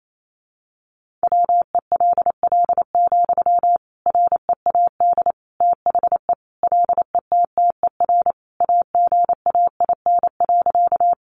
Morse code, heard as a keyed tone, about 28 wpm, 700 Hz. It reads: WELL, READ THE LETTER AGAIN.
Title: Sentences from Top 300 Words 28wpm